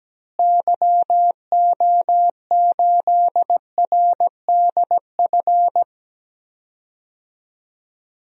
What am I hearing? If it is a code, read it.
Morse code, heard as a keyed tone, about 17 wpm, 700 Hz. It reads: YO8RDF
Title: DXCC-100 Call Signs - 6 Characters - Repeated 2x - 17wpm